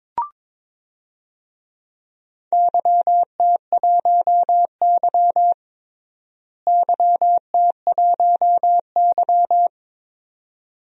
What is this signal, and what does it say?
Morse code, heard as a keyed tone, about 22 wpm, 700 Hz. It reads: YT1Y YT1Y
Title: DXCC-100 Call Signs - 4 Characters - Repeated 2x - 22wpm